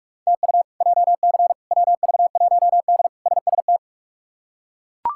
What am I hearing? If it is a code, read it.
Morse code, heard as a keyed tone, about 42 wpm, 700 Hz. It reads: TU JC WV1D SST
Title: K1USN SST QSO Practice - 45wpm